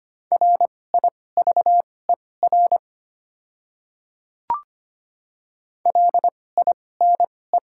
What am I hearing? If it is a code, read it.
Morse code, heard as a keyed tone, about 25 wpm, spaced as for 18 wpm, 700 Hz. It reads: RIVER LINE
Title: 2x Character Spacing - Top 500 Words - 25wpm